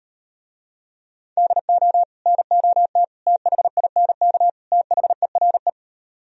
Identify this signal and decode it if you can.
Morse code, heard as a keyed tone, about 38 wpm, 700 Hz. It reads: DO NOT THINK THERE